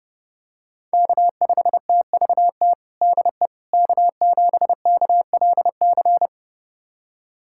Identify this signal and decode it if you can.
Morse code, heard as a keyed tone, about 30 wpm, 700 Hz. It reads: K5TVT DE K7KLC
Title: USA Callsign DE Callsign - 5 Characters - 30wpm